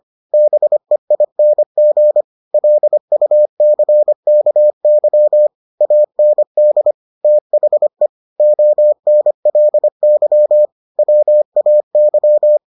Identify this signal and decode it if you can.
Morse code, heard as a keyed tone, about 25 wpm, 600 Hz. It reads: BEING LUCKY AND THE ONLY WAY